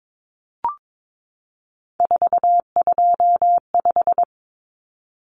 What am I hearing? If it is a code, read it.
Morse code, heard as a keyed tone, about 22 wpm, 700 Hz. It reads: 425
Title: Numbers - 3-Digit 22wpm